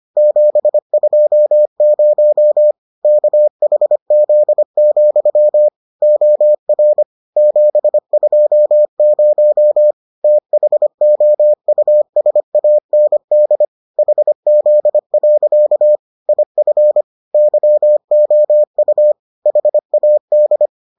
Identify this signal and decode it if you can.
Morse code, heard as a keyed tone, about 25 wpm, 600 Hz. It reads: 720 KHZ, OR 720 THOUSAND HZ. IF YOU HAD